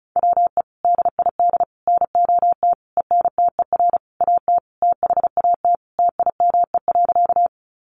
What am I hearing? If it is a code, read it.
Morse code, heard as a keyed tone, about 34 wpm, 700 Hz. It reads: WE DID NOT ENTER AT THAT TIME.
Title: Sentences from Top 1300 Words 35wpm